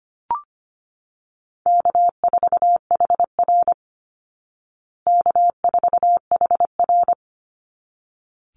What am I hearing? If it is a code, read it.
Morse code, heard as a keyed tone, about 25 wpm, 700 Hz. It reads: K4HR K4HR
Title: Call Signs Repeated 3x - 25wpm